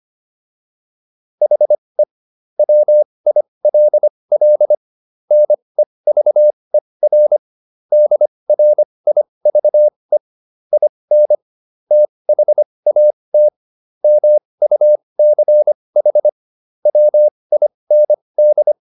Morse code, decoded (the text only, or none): HE WILL NEVER DRIVE IN THAT MUCH WIND